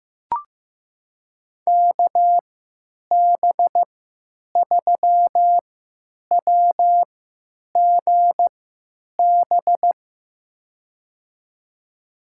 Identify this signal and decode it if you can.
Morse code, heard as a keyed tone, about 15 wpm, spaced as for 9 wpm, 700 Hz. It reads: KB3WGB
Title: USA Call Signs - 6 Characters - 3x Character Spacing - 15wpm